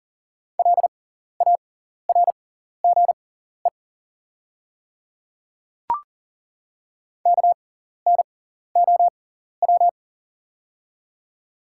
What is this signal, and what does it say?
Morse code, heard as a keyed tone, about 37 wpm, spaced as for 14 wpm, 700 Hz. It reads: LARGE KNOW
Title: 6x Character Spacing - Top 300 Words - 40wpm